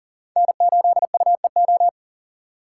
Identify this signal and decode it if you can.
Morse code, heard as a keyed tone, about 54 wpm, spaced as for 47 wpm, 700 Hz. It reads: N8UEO
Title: Speed Racing - USA 1x3 Call Signs 40wpm